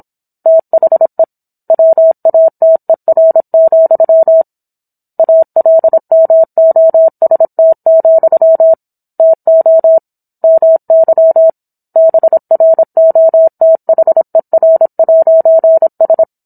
Morse code, heard as a keyed tone, about 27 wpm, 650 Hz. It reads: THE WATER, ALMOST, TO MY BROTHER'S